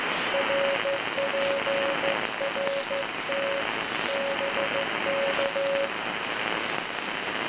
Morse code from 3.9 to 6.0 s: BK